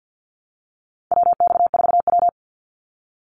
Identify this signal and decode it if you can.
Morse code, heard as a keyed tone, about 50 wpm, 700 Hz. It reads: WX4W
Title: Call Signs Repeated 2x - 50wpm